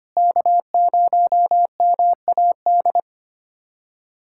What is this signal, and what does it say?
Morse code, heard as a keyed tone, about 25 wpm, 700 Hz. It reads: K0MAD
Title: USA Callsign DE Callsign - 5 Characters - 25wpm